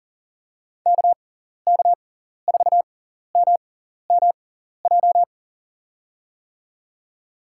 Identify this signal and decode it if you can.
Morse code, heard as a keyed tone, about 40 wpm, spaced as for 14 wpm, 700 Hz. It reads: KK4MMJ
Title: USA Call Signs - 6x Character Spacing - 40wpm